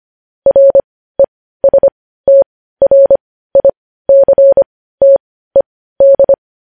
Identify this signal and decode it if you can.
Morse code, heard as a keyed tone, about 25 wpm, spaced as for 15 wpm, 550 Hz. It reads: RESTRICTED
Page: news headlines in morse code at 15 wpm